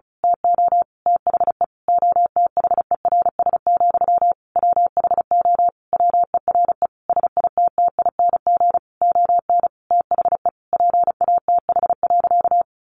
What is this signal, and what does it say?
Morse code, heard as a keyed tone, about 34 wpm, 700 Hz. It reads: TO THE OTHERS, WHO WERE SITTING ON THE PATH.
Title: A Sound Of Thunder 35wpm